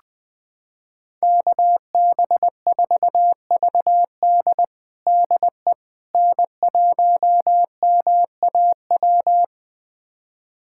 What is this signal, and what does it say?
Morse code, heard as a keyed tone, about 21 wpm, 700 Hz. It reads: KB4VD DE N1MAW
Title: USA Callsign DE Callsign - 5 Characters - 20wpm